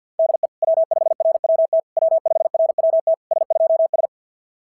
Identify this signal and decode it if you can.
Morse code, heard as a keyed tone, about 50 wpm, 650 Hz. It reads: DE W5RWT W5RWT S2S
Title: SOTA QSO Practice 50wpm